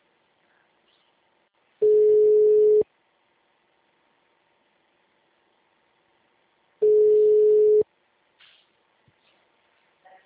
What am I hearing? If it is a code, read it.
Ringback tone (Europe, 425 Hz).
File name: out- -2013- - - .20440